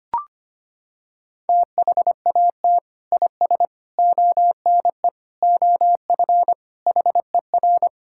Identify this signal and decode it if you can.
Morse code, heard as a keyed tone, about 25 wpm, 700 Hz. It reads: THAT IS ONE OF HER
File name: practice / 25wpm / Sentences with 2-Word Phrases - 25wpm